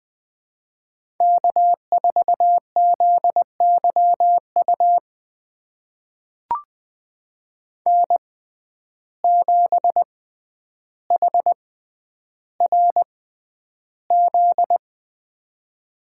Morse code, heard as a keyed tone, about 20 wpm, spaced as for 10 wpm, 700 Hz. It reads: K4ZYU N7HRZ
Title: USA Call Signs - 5 Characters - 6x Character Spacing - 20wpm